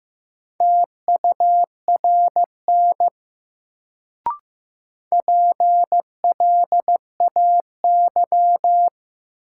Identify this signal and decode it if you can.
Morse code, heard as a keyed tone, about 15 wpm, 700 Hz. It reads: TURN PLAY
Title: Sentences from Top 200 Words - New Word Drill - 15wpm